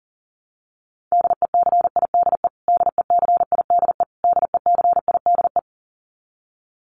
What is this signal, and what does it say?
Morse code, heard as a keyed tone, about 40 wpm, 700 Hz. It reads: DECIDE DECIDE DECIDE
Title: Top 500 Words Repeated 40wpm